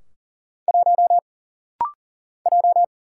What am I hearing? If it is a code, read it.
Morse code, heard as a keyed tone, about 40 wpm, 700 Hz. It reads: 1 J
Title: Single Letter-Number - Rapid-Fire 40wpm